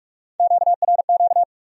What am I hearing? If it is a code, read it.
Morse code, heard as a keyed tone, about 58 wpm, 700 Hz. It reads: QRQ